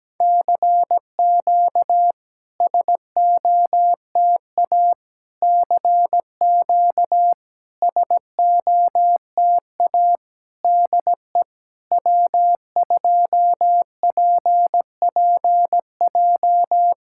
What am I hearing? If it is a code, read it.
Morse code, heard as a keyed tone, about 17 wpm, 700 Hz. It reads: CQ SOTA CQ SOTA DE W2PPJ
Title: SOTA QSO Practice 17wpm